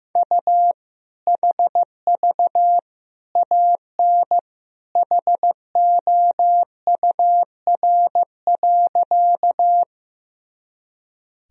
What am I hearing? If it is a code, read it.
Morse code, heard as a keyed tone, about 15 wpm, 700 Hz. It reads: U HV AN HOUR.
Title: Sentences from Top 400 Words and Abbreviations 15wpm